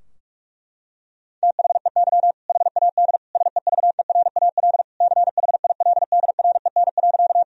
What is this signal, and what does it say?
Morse code, heard as a keyed tone, about 46 wpm, 700 Hz. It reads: THEY HAD SEVERAL CHILDREN.
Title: Sentences from Top 1000 Words - Review Part 2 - 45wpm